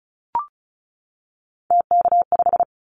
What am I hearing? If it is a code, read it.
Morse code, heard as a keyed tone, about 36 wpm, 700 Hz. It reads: TK5
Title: Callsign Prefixes for Top 100 DXCC - 35wpm